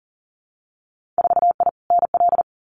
Morse code, heard as a keyed tone, about 38 wpm, 700 Hz. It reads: 4I NL
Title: Winter Field Day Exchanges Encore 40wpm